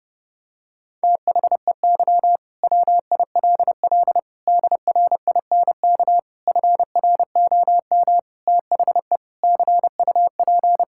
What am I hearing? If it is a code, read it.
Morse code, heard as a keyed tone, about 32 wpm, 700 Hz. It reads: THEY WILL DRINK FROM THE CUP
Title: Sentences from Top 1600 Words 30wpm